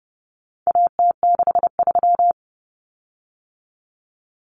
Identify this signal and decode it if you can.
Morse code, heard as a keyed tone, about 30 wpm, 700 Hz. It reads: AT63
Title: Common Cut Numbers - 4-Digit 30wpm